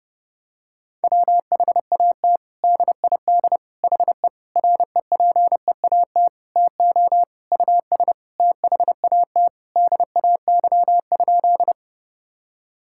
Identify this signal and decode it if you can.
Morse code, heard as a keyed tone, about 30 wpm, 700 Hz. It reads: WHAT DID HE REPEAT TO US THAT DAY?
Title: Sentences from Top 1600 Words 30wpm